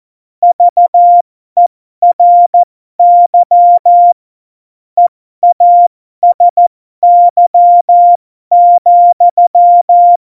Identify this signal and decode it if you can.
Morse code, heard as a keyed tone, about 14 wpm, spaced as for 12 wpm, 700 Hz. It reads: VERY EASY,
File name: Morsekurs / Witze